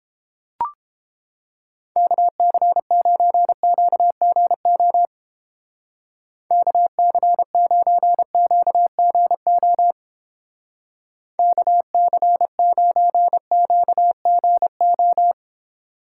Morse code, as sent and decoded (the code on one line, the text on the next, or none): -.- -.-. ----. --.- --. --- / -.- -.-. ----. --.- --. --- / -.- -.-. ----. --.- --. ---
KC9QGO KC9QGO KC9QGO